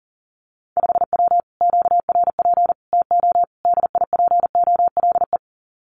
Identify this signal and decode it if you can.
Morse code, heard as a keyed tone, about 40 wpm, 700 Hz. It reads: 5W QRP TO DIPOLE
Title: Common QSO Phrases Encore 40wpm